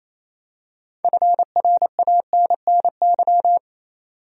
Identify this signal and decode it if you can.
Morse code, heard as a keyed tone, about 28 wpm, 700 Hz. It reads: FRANNY
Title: USA Common Names - 28wpm